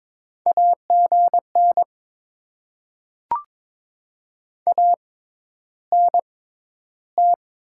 Morse code, heard as a keyed tone, about 22 wpm, spaced as for 11 wpm, 690 Hz. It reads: AGN ANT
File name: Top 20 QSO Elements - 6x Character Spacing - 22wpm